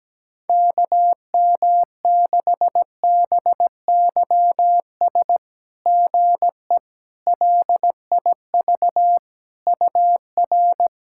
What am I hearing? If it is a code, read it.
Morse code, heard as a keyed tone, about 17 wpm, 690 Hz. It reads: KM6BYS GE LIV UR
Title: SOTA QSO Practice 17wpm